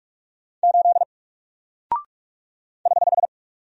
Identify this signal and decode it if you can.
Morse code, heard as a keyed tone, about 46 wpm, 700 Hz. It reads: 8 <HH>
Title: Single Character (Letter:Number:Punctuation:Prosign) 45wpm